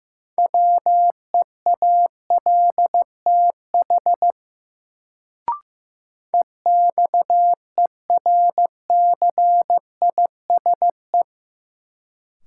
Morse code, heard as a keyed tone, about 15 wpm, 700 Hz. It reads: WEALTH EXERCISE
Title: Sentences from Top 1200 Words - New Word Drill Alt - 15wpm